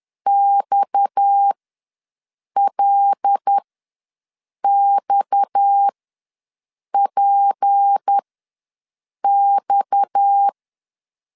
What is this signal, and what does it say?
Morse code, heard as a keyed tone, about 10 wpm, spaced as for 6 wpm, 800 Hz. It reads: XLXPX
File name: LEZIONI / 6 PKX0 / 6 PKX ws 10wpm 3706